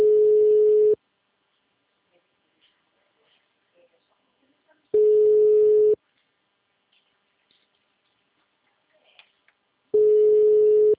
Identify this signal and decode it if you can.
Ringback tone (Europe, 425 Hz), with speech.